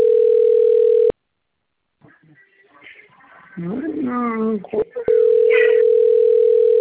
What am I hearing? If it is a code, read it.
Ringback tone (North America), 440 + 480 Hz, with speech.
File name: rg-1600-unknown- - - .73440